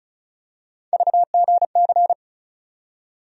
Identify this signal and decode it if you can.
Morse code, heard as a keyed tone, about 36 wpm, 700 Hz. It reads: VGC
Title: USA Call Sign Postfixes - 3 Letters 35wpm